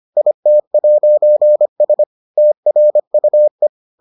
Morse code, heard as a keyed tone, about 25 wpm, 600 Hz. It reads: IT'S TRUE